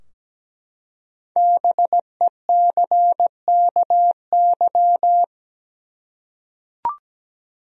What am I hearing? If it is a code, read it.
Morse code, heard as a keyed tone, about 17 wpm, 700 Hz. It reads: BECKY
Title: USA Common Names - Max 5 Letters - 17wpm